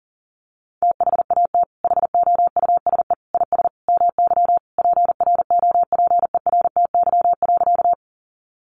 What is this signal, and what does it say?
Morse code, heard as a keyed tone, about 40 wpm, 700 Hz. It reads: THAT HOUSE IS MY PROPERTY.
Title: Sentences from Top 1000 Words 40wpm